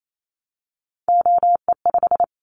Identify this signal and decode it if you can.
Morse code, heard as a keyed tone, about 28 wpm, 700 Hz. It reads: OE5
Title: Callsign Prefixes for Top 25 DXCC - Repeated 3x - 28wpm